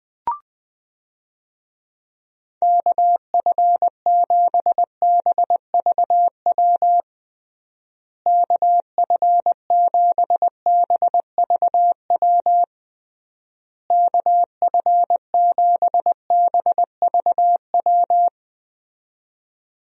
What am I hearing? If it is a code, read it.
Morse code, heard as a keyed tone, about 20 wpm, 700 Hz. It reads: KF7BVW KF7BVW KF7BVW
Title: USA Call Signs - 6 Characters - Repeated 3x - 20wpm